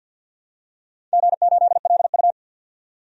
Morse code, heard as a keyed tone, about 51 wpm, 700 Hz. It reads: G8LU
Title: DXCC-100 Call Signs - 4 Characters - 50wpm